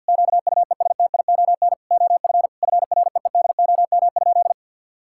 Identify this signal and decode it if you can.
Morse code, heard as a keyed tone, about 50 wpm, 700 Hz. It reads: QUESTION OF FREEDOM?
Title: Sentences from Top 1600 Words 50wpm